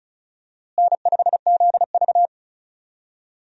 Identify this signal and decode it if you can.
Morse code, heard as a keyed tone, about 36 wpm, 700 Hz. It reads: N5ZV
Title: USA Call Signs - 4 Characters - Repeated 3x - 35wpm